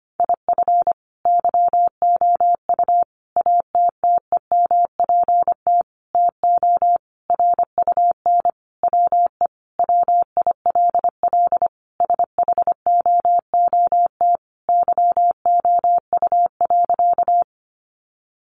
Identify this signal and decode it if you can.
Morse code, heard as a keyed tone, about 25 wpm, 700 Hz. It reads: IF YOU ATTEMPT TO RUN WE WILL SHOOT YOU.
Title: Sentences from Top 1200 Words 25wpm